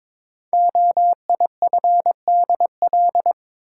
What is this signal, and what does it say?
Morse code, heard as a keyed tone, about 22 wpm, 700 Hz. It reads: OIFDL
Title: Code Groups - 5 Letters - 22wpm